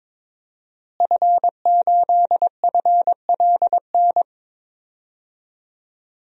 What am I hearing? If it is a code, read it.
Morse code, heard as a keyed tone, about 22 wpm, 700 Hz. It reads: F8FLN
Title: DXCC-100 Call Signs - 5 Characters - Repeated 2x - 22wpm